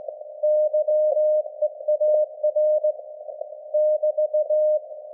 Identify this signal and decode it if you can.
Morse code, heard as a keyed tone, about 19 wpm, 610 Hz. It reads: YEAR <BT>
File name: laengstwellenempfang / SDRSharp 17kHz AF